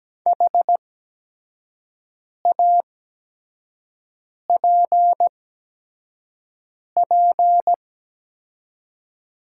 Morse code, heard as a keyed tone, about 17 wpm, spaced as for 5 wpm, 700 Hz. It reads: HAPP
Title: 8x Character Spacing - Top 500 Words - 17wpm